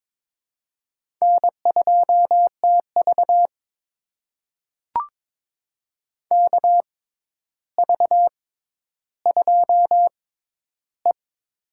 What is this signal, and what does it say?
Morse code, heard as a keyed tone, about 22 wpm, spaced as for 11 wpm, 700 Hz. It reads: N2TV KV2E
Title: USA Call Signs - 4 Characters - 6x Character Spacing - 22wpm